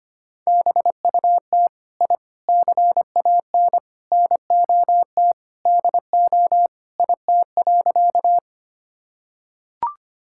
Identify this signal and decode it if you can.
Morse code, heard as a keyed tone, about 25 wpm, 700 Hz. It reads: BUT I CAN NOT DO IT.
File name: practice / 25wpm / Sentences with 4-Word Phrases - 25wpm